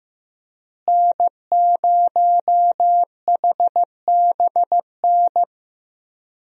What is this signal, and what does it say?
Morse code, heard as a keyed tone, about 15 wpm, 700 Hz. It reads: N0HBN
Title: USA Call Signs - 5 Characters - Repeated 3x - 15wpm